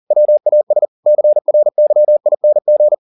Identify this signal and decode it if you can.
Morse code, heard as a keyed tone, about 40 wpm, 600 Hz. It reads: WAS CRYING
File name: the old man and the sea 2 40wpm